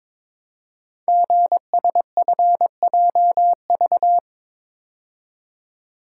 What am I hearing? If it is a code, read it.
Morse code, heard as a keyed tone, about 22 wpm, 700 Hz. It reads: GSFJV
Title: Code Groups - 5 Letters - 22wpm